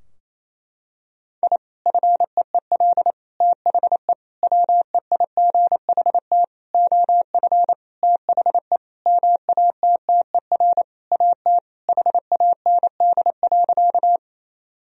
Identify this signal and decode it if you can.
Morse code, heard as a keyed tone, about 28 wpm, 700 Hz. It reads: I FEEL THE WEIGHT OF THE MATTER AT HAND.
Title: Sentences from Top 1000 Words - Review Part 3 - 28wpm